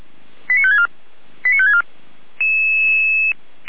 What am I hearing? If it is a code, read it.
Five-tone selective call (ZVEI), heard as quick runs of five tones, tones of about 70 ms. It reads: 89564 89564
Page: SDR/ZVEI